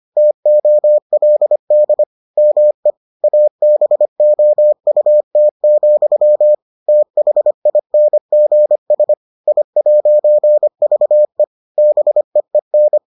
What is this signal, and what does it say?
Morse code, heard as a keyed tone, about 25 wpm, 600 Hz. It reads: TOLD ME ABOUT, THINGS I'VE BEEN